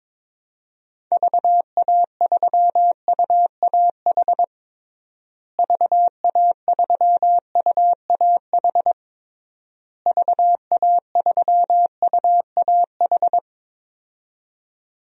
Morse code, heard as a keyed tone, about 22 wpm, 700 Hz. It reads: VA3UAH VA3UAH VA3UAH
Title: DXCC-100 Call Signs - 6 Characters - Repeated 3x - 22wpm